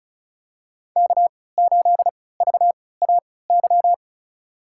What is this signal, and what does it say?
Morse code, heard as a keyed tone, about 34 wpm, spaced as for 20 wpm, 700 Hz. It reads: K8VAY
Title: USA Call Signs - 5 Characters - 3x Character Spacing - 35wpm